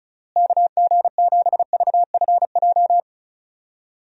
Morse code, heard as a keyed tone, about 36 wpm, 700 Hz. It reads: KG7VFJ
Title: USA Call Signs - 6 Characters - Repeated 3x - 35wpm